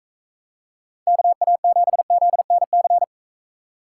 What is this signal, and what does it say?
Morse code, heard as a keyed tone, about 44 wpm, 700 Hz. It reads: KA7ZNC